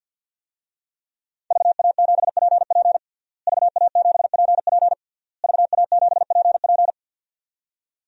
Morse code, heard as a keyed tone, about 52 wpm, 700 Hz. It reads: VA7PP VA7PP VA7PP
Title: DXCC-100 Call Signs - 5 Characters - Repeated 3x - 50wpm